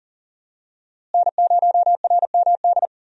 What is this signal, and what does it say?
Morse code, heard as a keyed tone, about 39 wpm, 700 Hz. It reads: N0RMD
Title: USA Call Signs - 5 Characters - Repeated 3x - 40wpm